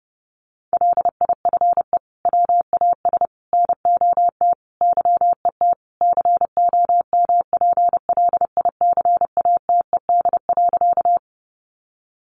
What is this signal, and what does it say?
Morse code, heard as a keyed tone, about 30 wpm, 700 Hz. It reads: LIFE WAS NOT YET COMPLICATED.
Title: Sentences from Top 1200 Words 30wpm